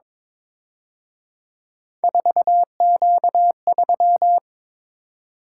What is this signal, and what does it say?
Morse code, heard as a keyed tone, about 26 wpm, 700 Hz. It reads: 4Q3
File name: Speed-Racing - Three Letter-Number Combinations - 22wpm